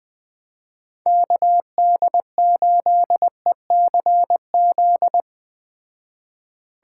Morse code, heard as a keyed tone, about 20 wpm, 700 Hz. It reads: KD8ECZ
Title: Speed Racing - USA 2x3 Call Signs 17wpm